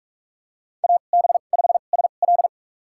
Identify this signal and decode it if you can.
Morse code, heard as a keyed tone, about 46 wpm, spaced as for 33 wpm, 700 Hz. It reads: AB5SL
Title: USA Call Signs - 5 Characters - 2x Character Spacing - 45wpm